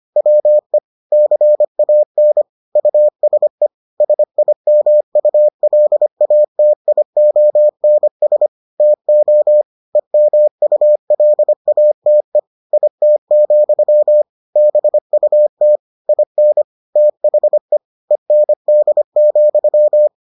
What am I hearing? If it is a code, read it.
Morse code, heard as a keyed tone, about 25 wpm, 600 Hz. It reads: WE CAN USE SIMULATIONS TO EMULATE IT, BUT IN THE END,